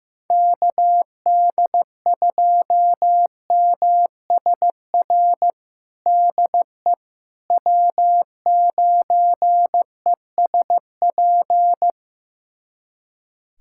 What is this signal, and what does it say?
Morse code, heard as a keyed tone, about 15 wpm, 700 Hz. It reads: KD2MSR DE W9ESP
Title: USA Callsign DE Callsign 15wpm